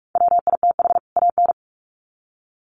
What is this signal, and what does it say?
Morse code, heard as a keyed tone, about 44 wpm, 700 Hz. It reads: WITH AN
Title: Sets of 2 Words 45wpm